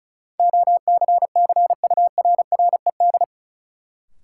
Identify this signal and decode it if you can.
Morse code, heard as a keyed tone, about 36 wpm, 700 Hz. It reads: OCCURRED
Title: Sets of 5 Words Encore - New Word Drill - 35wpm